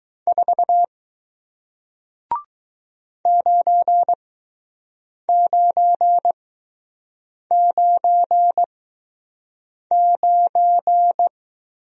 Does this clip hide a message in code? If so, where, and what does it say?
Yes: Morse code from 0.3 to 11.3 s, heard as a keyed tone, about 19 wpm, spaced as for 6 wpm, 700 Hz. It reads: 4 9999